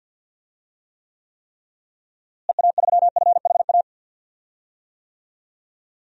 Morse code, heard as a keyed tone, about 52 wpm, 700 Hz. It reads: EA3FHA
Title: DXCC-100 Call Signs - 6 Characters - Repeated 3x - 50wpm